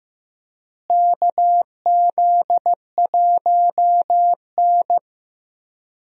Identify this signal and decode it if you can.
Morse code, heard as a keyed tone, about 15 wpm, 700 Hz. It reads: KZ1N
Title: USA Call Signs - 4 Characters - Repeated 2x - 15wpm